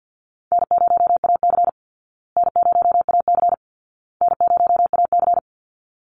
Morse code, heard as a keyed tone, about 50 wpm, 700 Hz. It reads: N0AC N0AC N0AC
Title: Easy Call Signs Repeated 3x - 50wpm